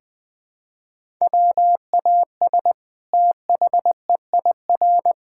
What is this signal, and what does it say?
Morse code, heard as a keyed tone, about 20 wpm, 700 Hz. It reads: WAS THEIR
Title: Sentences from Top 1000 Words - Review Part 5 - 20wpm